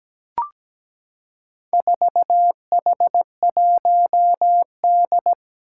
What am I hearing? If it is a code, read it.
Morse code, heard as a keyed tone, about 17 wpm, 700 Hz. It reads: 4H1D
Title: Four Challenging Letter-Number Combinations 17wpm